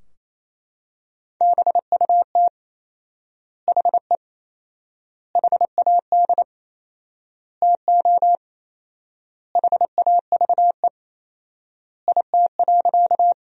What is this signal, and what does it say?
Morse code, heard as a keyed tone, about 28 wpm, 700 Hz. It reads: BUT HE HAD TO HAVE IT.
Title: 4x Word-Spacing - Sentences from Top 100 Words - 28wpm